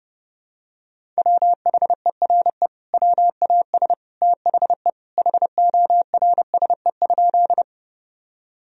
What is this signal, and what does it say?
Morse code, heard as a keyed tone, about 30 wpm, 700 Hz. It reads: WHERE WAS THE HORSE?
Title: Sentences from Top 1000 Words - Review Part 1 - 30wpm